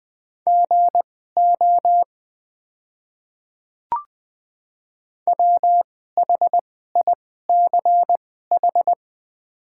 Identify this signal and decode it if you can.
Morse code, heard as a keyed tone, about 21 wpm, spaced as for 15 wpm, 700 Hz. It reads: GO WHICH
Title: 2x Character Spacing - Top 100 Words - 20wpm